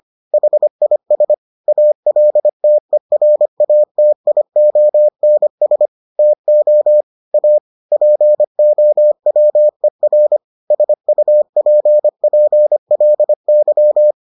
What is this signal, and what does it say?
Morse code, heard as a keyed tone, about 25 wpm, 600 Hz. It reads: HIS ALTERATIONS TO A POWER SUPPLY